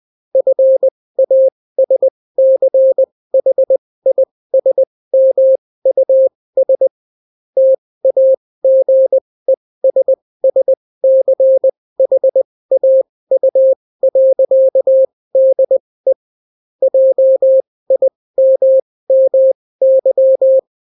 Morse code, heard as a keyed tone, about 20 wpm, spaced as for 16 wpm, 530 Hz. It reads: FASCHISMUS TAGESSCHAU.DE JIMMY